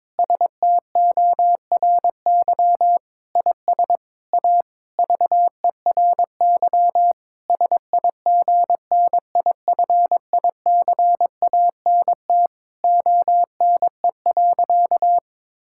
Morse code, heard as a keyed tone, about 22 wpm, 700 Hz. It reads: STORY IS A VERY SIGNIFICANT ONE.
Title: Sentences from Top 1000 Words - Review Part 6 - 22wpm